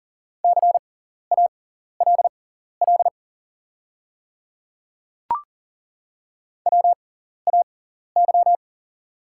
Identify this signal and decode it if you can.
Morse code, heard as a keyed tone, about 40 wpm, spaced as for 14 wpm, 690 Hz. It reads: CALL WAY
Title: 6x Character Spacing - Top 100 Words - 40wpm